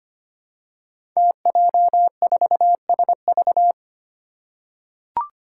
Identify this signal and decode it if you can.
Morse code, heard as a keyed tone, about 25 wpm, 700 Hz. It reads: TJ4SV
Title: Five Challenging Letter-Number Combinations 25wpm